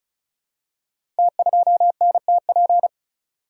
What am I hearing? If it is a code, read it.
Morse code, heard as a keyed tone, about 33 wpm, 690 Hz. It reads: T2NTP
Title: Code Groups - 5 Letter:Numbers - 35wpm